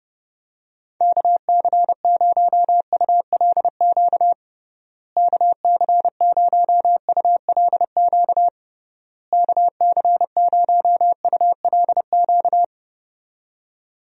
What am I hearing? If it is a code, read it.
Morse code, heard as a keyed tone, about 32 wpm, 700 Hz. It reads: KC0ULQ KC0ULQ KC0ULQ
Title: USA Call Signs - 6 Characters - Repeated 3x - 30wpm